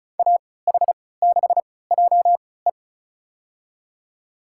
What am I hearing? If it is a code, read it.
Morse code, heard as a keyed tone, about 36 wpm, spaced as for 20 wpm, 700 Hz. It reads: AH6JE
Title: USA Call Signs - 5 Characters - 3x Character Spacing - 35wpm